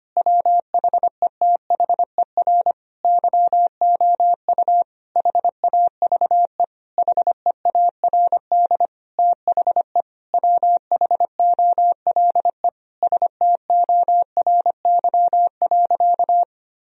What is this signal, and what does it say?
Morse code, heard as a keyed tone, about 25 wpm, 700 Hz. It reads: WHETHER YOU HAVE HEARD THE WHOLE STORY.
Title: Sentences from Top 1000 Words - Review Part 3 - 25wpm